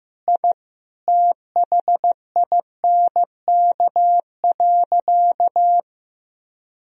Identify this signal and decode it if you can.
Morse code, heard as a keyed tone, about 15 wpm, 700 Hz. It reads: I THINK.
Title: Sentences from Top 1000 Words - Review Part 2 - 15wpm